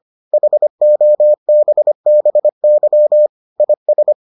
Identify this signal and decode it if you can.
Morse code, heard as a keyed tone, about 25 wpm, 600 Hz. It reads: HOBBY IS